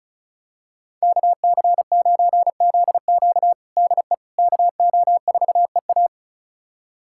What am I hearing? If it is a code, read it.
Morse code, heard as a keyed tone, about 36 wpm, 700 Hz. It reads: KC9ZQ DE KO4EA
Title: USA Callsign DE Callsign - 5 Characters - 35wpm